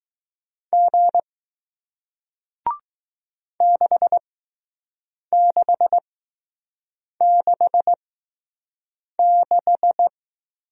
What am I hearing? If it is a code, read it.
Morse code, heard as a keyed tone, about 20 wpm, spaced as for 6 wpm, 700 Hz. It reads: G 6666